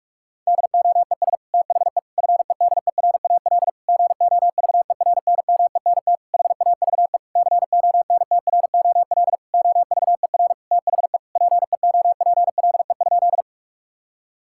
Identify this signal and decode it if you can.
Morse code, heard as a keyed tone, about 46 wpm, 700 Hz. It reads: DOES THE FEDERAL GOVERNMENT HAVE CONTROL OVER THE PEOPLE?
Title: Sentences from Top 1000 Words - Review Part 5 - 45wpm